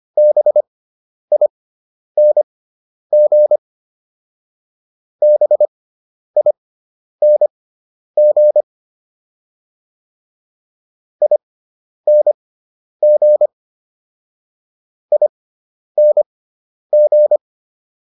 Morse code, heard as a keyed tone, about 25 wpm, spaced as for 10 wpm, 600 Hz. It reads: BING BING ING ING